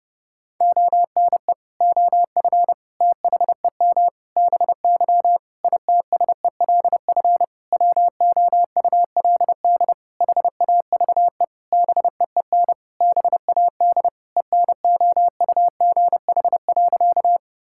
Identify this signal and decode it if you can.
Morse code, heard as a keyed tone, about 30 wpm, 700 Hz. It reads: ONE OF THEM BY ITSELF WOULD HAVE BEEN BAD ENOUGH.
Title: Sentences from Top 1000 Words - Review Part 5 - 30wpm